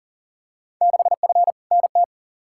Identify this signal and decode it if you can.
Morse code, heard as a keyed tone, about 40 wpm, 700 Hz. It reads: 6F NT
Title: Field Day Exchanges Encore - 40wpm